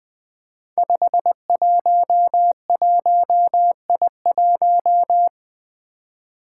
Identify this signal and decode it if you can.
Morse code, heard as a keyed tone, about 20 wpm, 700 Hz. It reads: 511I1
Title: Five Challenging Letter-Number Combinations 20wpm